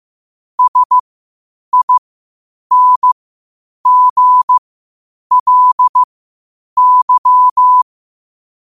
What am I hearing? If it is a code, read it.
Morse code, heard as a keyed tone, about 15 wpm, spaced as for 8 wpm, 1000 Hz. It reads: SINGLY
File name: qotdfiles / 8wpm